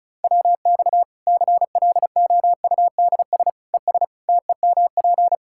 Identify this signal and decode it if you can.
Morse code, heard as a keyed tone, about 36 wpm, 700 Hz. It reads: WX CLOUDS ES TEMP
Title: Practice QSO - Exchange 2 - Info - 35wpm